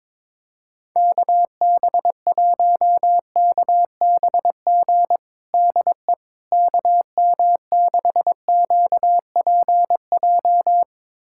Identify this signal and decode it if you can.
Morse code, heard as a keyed tone, about 22 wpm, 700 Hz. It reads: KB1KBG DE KM6QPJ
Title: USA Callsign DE Callsign 22wpm